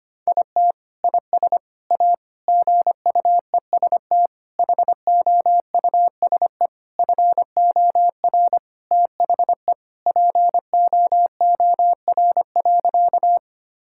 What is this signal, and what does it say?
Morse code, heard as a keyed tone, about 25 wpm, 700 Hz. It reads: IT IS A GUEST HOUSE FOR THE POOR.
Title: Sentences from Top 1600 Words 25wpm